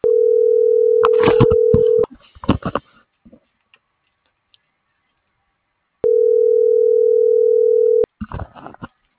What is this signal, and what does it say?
Ringback tone (North America), 440 + 480 Hz.